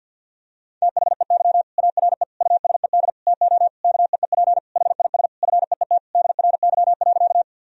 Morse code, heard as a keyed tone, about 52 wpm, 700 Hz. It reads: THEY ARE USED TO KEEP HIS FEET DRY.
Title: Sentences from Top 1000 Words - Review Part 3 - 50wpm